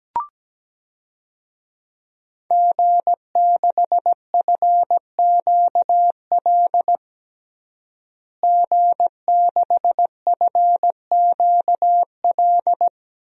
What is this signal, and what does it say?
Morse code, heard as a keyed tone, about 17 wpm, 700 Hz. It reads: G6FQL G6FQL
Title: DXCC-100 Call Signs - 5 Characters - Repeated 2x - 17wpm